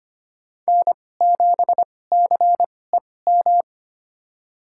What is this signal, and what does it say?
Morse code, heard as a keyed tone, about 25 wpm, spaced as for 18 wpm, 700 Hz. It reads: N7CEM
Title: USA Call Signs - 5 Characters - 2x Character Spacing - 25wpm